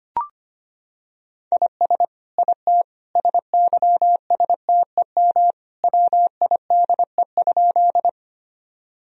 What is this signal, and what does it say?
Morse code, heard as a keyed tone, about 25 wpm, 700 Hz. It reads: IS IT SYSTEM WIDE?
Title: Sentences from Top 1000 Words - Review Part 3 - 25wpm